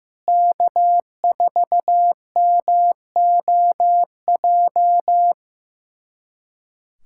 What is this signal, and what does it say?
Morse code, heard as a keyed tone, about 15 wpm, 700 Hz. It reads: K4MOJ
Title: USA Callsign DE Callsign - 5 Characters - 15wpm